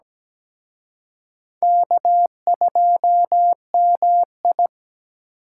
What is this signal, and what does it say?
Morse code, heard as a keyed tone, about 20 wpm, 700 Hz. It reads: K2MI
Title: Speed Racing - USA 1x2 Call Signs 17wpm